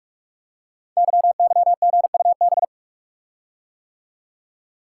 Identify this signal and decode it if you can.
Morse code, heard as a keyed tone, about 44 wpm, 690 Hz. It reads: YYGUB
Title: Code Groups - 5 Letters - 45wpm